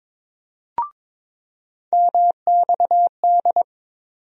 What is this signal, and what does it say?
Morse code, heard as a keyed tone, about 22 wpm, 700 Hz. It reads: MXD
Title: USA Call Sign Postfixes - 3 Letters 22wpm